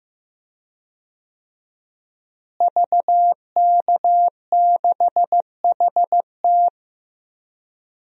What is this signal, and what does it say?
Morse code, heard as a keyed tone, about 15 wpm, 700 Hz. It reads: VK6HT
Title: DXCC-100 Call Signs - 5 Characters - Repeated 3x - 15wpm